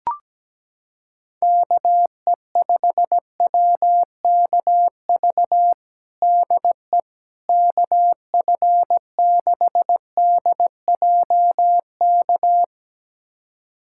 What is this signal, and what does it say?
Morse code, heard as a keyed tone, about 17 wpm, 700 Hz. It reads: KE5WKV DE KF6DJK
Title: USA Callsign DE Callsign - 6 Characters - 17wpm